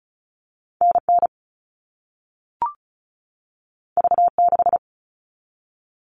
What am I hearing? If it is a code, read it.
Morse code, heard as a keyed tone, about 36 wpm, 700 Hz. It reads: NN V6
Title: All Cut Numbers - 2-Digit 35wpm